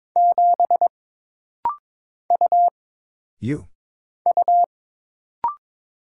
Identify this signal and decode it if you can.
Morse code, heard as a keyed tone, about 22 wpm, 700 Hz. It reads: 7 U U
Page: This rapid-fire timing is geared to push your ICR (Instant Character Recognition) skills.